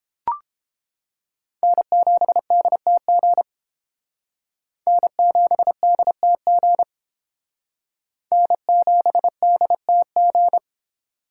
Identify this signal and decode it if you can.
Morse code, heard as a keyed tone, about 28 wpm, 700 Hz. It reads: N7DTG N7DTG N7DTG